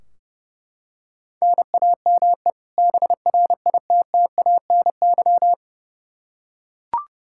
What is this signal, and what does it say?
Morse code, heard as a keyed tone, about 31 wpm, 700 Hz. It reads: NAME BRITTANY
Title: K1USN SST QSO Practice 30wpm